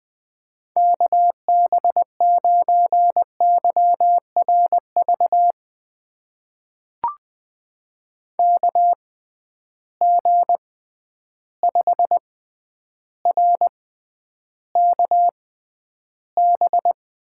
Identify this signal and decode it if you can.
Morse code, heard as a keyed tone, about 20 wpm, spaced as for 10 wpm, 700 Hz. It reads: KB9YRV KG5RKB
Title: USA Call Signs - 6 Characters - 6x Character Spacing - 20wpm